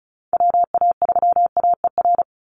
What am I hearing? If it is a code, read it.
Morse code, heard as a keyed tone, about 34 wpm, 700 Hz. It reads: WA3AER
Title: Call Signs - 35wpm